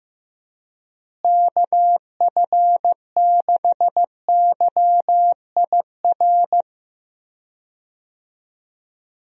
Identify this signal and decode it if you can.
Morse code, heard as a keyed tone, about 15 wpm, 700 Hz. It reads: KF6YIR
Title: USA Call Signs - 6 Characters - Repeated 2x - 15wpm